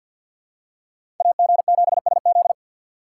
Morse code, heard as a keyed tone, about 47 wpm, 700 Hz. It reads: AG7SZ
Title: USA Call Signs - 5 Characters - 4x Character Spacing - 50wpm